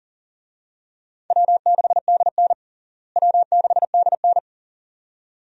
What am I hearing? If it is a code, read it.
Morse code, heard as a keyed tone, about 37 wpm, 700 Hz. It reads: W6DN W6DN
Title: USA Call Signs - 4 Characters - Repeated 2x - 40wpm